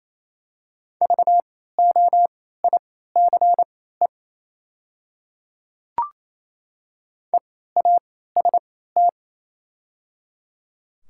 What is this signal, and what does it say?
Morse code, heard as a keyed tone, about 28 wpm, spaced as for 16 wpm, 700 Hz. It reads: VOICE EAST
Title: 3x Character Spacing - Top 500 Words - 28wpm